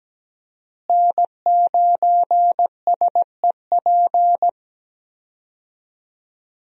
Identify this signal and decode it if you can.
Morse code, heard as a keyed tone, about 17 wpm, 700 Hz. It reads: N9SEP
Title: USA Call Signs - 5 Characters - Repeated 2x - 17wpm